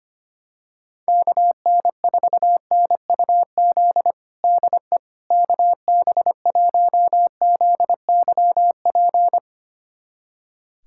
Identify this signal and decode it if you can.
Morse code, heard as a keyed tone, about 25 wpm, 700 Hz. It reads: KN4NUZ DE KB1ZYP
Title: USA Callsign DE Callsign 25wpm